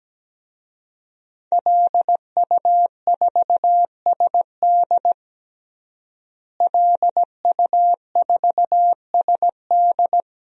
Morse code, heard as a keyed tone, about 17 wpm, 700 Hz. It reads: LU4SD LU4SD
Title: DXCC-100 Call Signs - 5 Characters - Repeated 2x - 17wpm